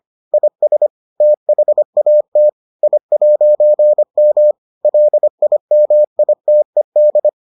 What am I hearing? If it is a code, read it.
Morse code, heard as a keyed tone, about 25 wpm, 600 Hz. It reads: IS THAT I'M LIMITED